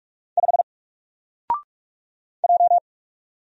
Morse code, heard as a keyed tone, about 45 wpm, 700 Hz. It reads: 5 J
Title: Single Letter-Number 45wpm